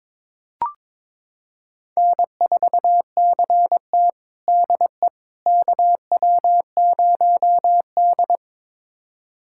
Morse code, heard as a keyed tone, about 22 wpm, 700 Hz. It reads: N4CT DE KW0D
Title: USA Callsign DE Callsign - 4 Characters - 22wpm